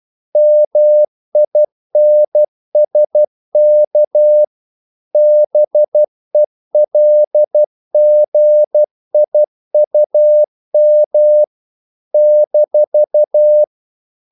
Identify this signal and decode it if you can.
Morse code, heard as a keyed tone, about 12 wpm, 600 Hz. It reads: MINSK BELGIUM -
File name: CW / Practice / NationalCapitals12